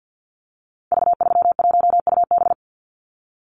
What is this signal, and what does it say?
Morse code, heard as a keyed tone, about 50 wpm, 700 Hz. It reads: V31UB
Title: Call Signs Repeated 2x - 50wpm